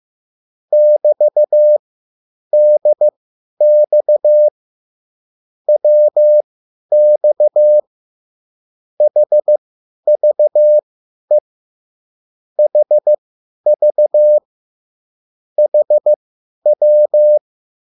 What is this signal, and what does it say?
Morse code, heard as a keyed tone, about 15 wpm, spaced as for 10 wpm, 600 Hz. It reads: <BT>DX WX HVE HV HW
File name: QSL / CW lesson